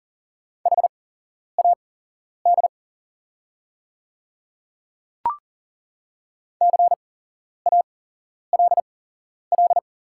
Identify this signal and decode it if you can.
Morse code, heard as a keyed tone, about 40 wpm, spaced as for 11 wpm, 700 Hz. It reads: HAD CALL
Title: 8x Character Spacing - Top 100 Words - 40wpm